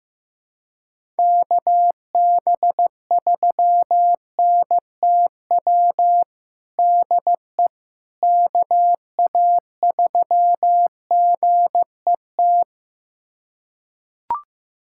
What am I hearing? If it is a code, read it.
Morse code, heard as a keyed tone, about 15 wpm, 700 Hz. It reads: KB3NTW DE KA3GET
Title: USA Callsign DE Callsign - 6 Characters - 15wpm